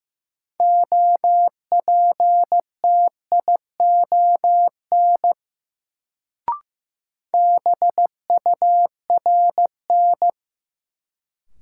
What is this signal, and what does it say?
Morse code, heard as a keyed tone, about 15 wpm, 700 Hz. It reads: OPTION BURN
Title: Sentences from Top 1300 Words - New Word Drill Alt - 15wpm